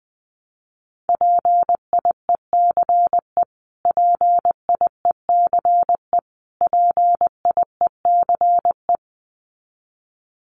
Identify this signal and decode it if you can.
Morse code, heard as a keyed tone, about 20 wpm, 700 Hz. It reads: PIECE PIECE PIECE
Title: Top 500 Words Repeated 20wpm